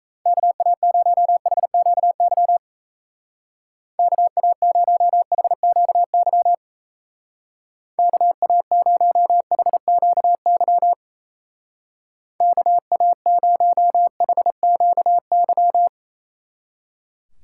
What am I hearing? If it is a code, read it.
Morse code, heard as a keyed tone, about 36 wpm, 700 Hz. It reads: KA0HQY KA0HQY KA0HQY KA0HQY